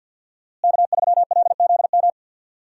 Morse code, heard as a keyed tone, about 52 wpm, 690 Hz. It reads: K3LZM
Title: USA Call Signs - 5 Characters - Repeated 3x - 50wpm